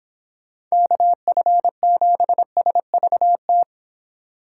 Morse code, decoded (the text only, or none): KF7SVT